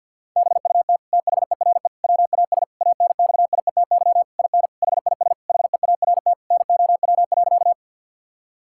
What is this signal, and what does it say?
Morse code, heard as a keyed tone, about 50 wpm, 700 Hz. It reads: BUT THERE WAS ANXIETY IN HIS HEART NOW.
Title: Sentences from Top 1600 Words 50wpm